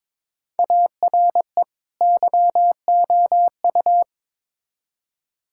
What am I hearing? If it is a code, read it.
Morse code, heard as a keyed tone, about 22 wpm, 700 Hz. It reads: ARE YOU
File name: Build A Phrase - 2-Word Phrases - 22wpm